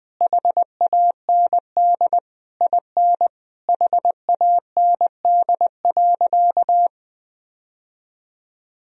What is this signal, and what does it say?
Morse code, heard as a keyed tone, about 20 wpm, 700 Hz. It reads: HAND IN HAND.
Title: Sentences from Top 1000 Words - Review Part 3 - 20wpm